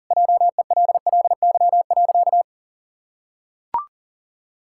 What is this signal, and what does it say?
Morse code, heard as a keyed tone, about 40 wpm, 700 Hz. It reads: JELLY.
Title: Pangram Challenge v2 - 40wpm